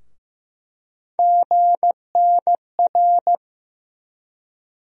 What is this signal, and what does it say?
Morse code, heard as a keyed tone, about 16 wpm, 700 Hz. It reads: GNR
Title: USA Call Sign Postfixes - 3 Letters 15wpm